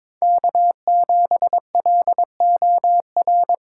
Morse code, heard as a keyed tone, about 22 wpm, 700 Hz. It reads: K7LOR
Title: USA Call Signs - 5 Characters - 8x Character Spacing - 22wpm